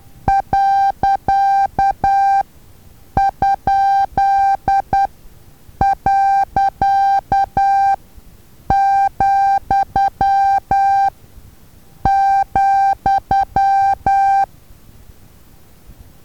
Morse code, heard as a keyed tone, about 10 wpm, spaced as for 7 wpm, 790 Hz. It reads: .?.,,
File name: Taviro HA6VA A